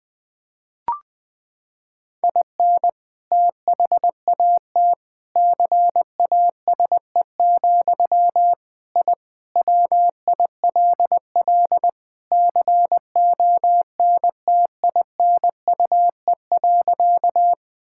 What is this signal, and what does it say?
Morse code, heard as a keyed tone, about 21 wpm, 700 Hz. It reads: IN THAT CASE, I WILL CONTINUE.
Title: Sentences from Top 1000 Words - Review Part 6 - 20wpm